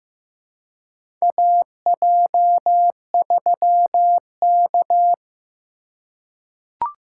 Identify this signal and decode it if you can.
Morse code, heard as a keyed tone, about 16 wpm, 700 Hz. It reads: AJ3K
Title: USA Call Signs - 2x Character Spacing - 15wpm